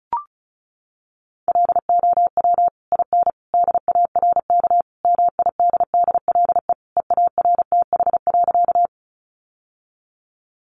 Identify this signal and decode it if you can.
Morse code, heard as a keyed tone, about 34 wpm, 700 Hz. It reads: LOW IN DARK MIDDLE EARTH.
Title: Sentences from Top 900 Words 35wpm